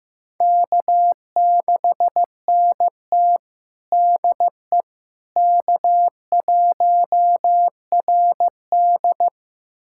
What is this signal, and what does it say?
Morse code, heard as a keyed tone, about 15 wpm, 700 Hz. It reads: K6NT DE K1RD
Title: USA Callsign DE Callsign - 4 Characters - 15wpm